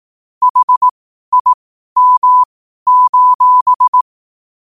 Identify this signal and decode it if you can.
Morse code, heard as a keyed tone, about 18 wpm, spaced as for 12 wpm, 1000 Hz. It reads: HIM:
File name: qotdfiles / 12wpm